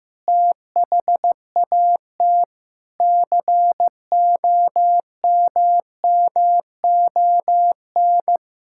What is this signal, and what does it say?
Morse code, heard as a keyed tone, about 14 wpm, 700 Hz. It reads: THAT COMMON
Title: Sentences from Top 1000 Words - Review Part 5 - 15wpm